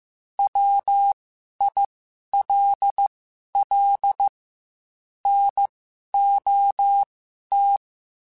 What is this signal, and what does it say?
Morse code, heard as a keyed tone, about 15 wpm, spaced as for 11 wpm, 800 Hz. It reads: WILL NOT
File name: Quotes / Quotes-mp3s / IgnoringFacts @13